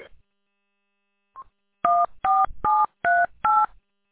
DTMF, heard as telephone touch tones, tones of about 220 ms. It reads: 14738
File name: w8cmn- -combined